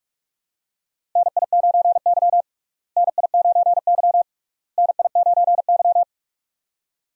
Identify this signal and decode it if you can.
Morse code, heard as a keyed tone, about 41 wpm, 700 Hz. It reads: NI9Y NI9Y NI9Y
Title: USA Call Signs - 4 Characters - Repeated 3x - 45wpm